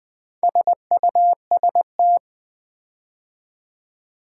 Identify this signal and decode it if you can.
Morse code, heard as a keyed tone, about 20 wpm, 700 Hz. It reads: SUST
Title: Build A Word - 4 Letters At A Time - 11-Letter Words - 20wpm